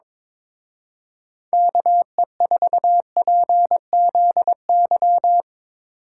Morse code, heard as a keyed tone, about 22 wpm, 700 Hz. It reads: KE4PZY
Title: Speed Racing - USA 2x3 Call Signs 22wpm